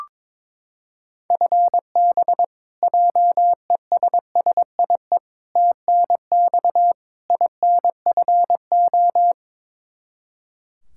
Morse code, heard as a keyed tone, about 22 wpm, 700 Hz. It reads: FB JESSIE TNX INFO
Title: Practice QSO - Exchange 3 - More Info - 22wpm